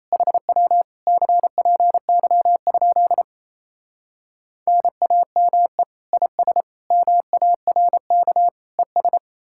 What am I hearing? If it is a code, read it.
Morse code, heard as a keyed tone, about 37 wpm, 700 Hz. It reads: HW CPY? NAME IS MARK ES